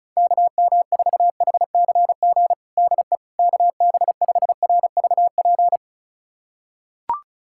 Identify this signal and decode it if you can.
Morse code, heard as a keyed tone, about 36 wpm, 700 Hz. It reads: KM4HCG DE KB5RVP
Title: USA Callsign DE Callsign 35wpm